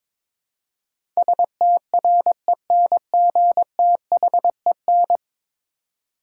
Morse code, heard as a keyed tone, about 22 wpm, 700 Hz. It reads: STRENGTHEN
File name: practice / 22wpm / 10 Letter Words - First 3 Letters Given - 22wpm